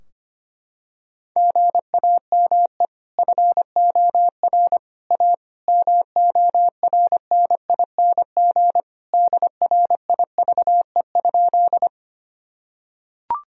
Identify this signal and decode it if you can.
Morse code, heard as a keyed tone, about 25 wpm, 700 Hz. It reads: GAME FOR A MORNING DRIVE?
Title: Sentences from Top 1000 Words - Review Part 3 - 25wpm